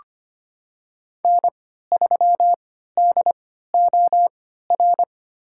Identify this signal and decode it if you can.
Morse code, heard as a keyed tone, about 25 wpm, spaced as for 14 wpm, 700 Hz. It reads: N3DOR
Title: USA Call Signs - 5 Characters - 3x Character Spacing - 25wpm